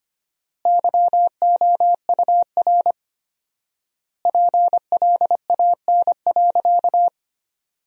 Morse code, heard as a keyed tone, about 25 wpm, 700 Hz. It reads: YOUR PLAN.
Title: 4x Word-Spacing - Sentences from Top 500 Words - 25wpm